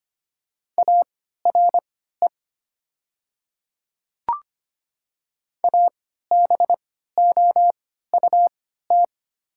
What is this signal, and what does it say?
Morse code, heard as a keyed tone, about 25 wpm, spaced as for 14 wpm, 700 Hz. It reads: ARE ABOUT
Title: 3x Character Spacing - Top 200 Words - 25wpm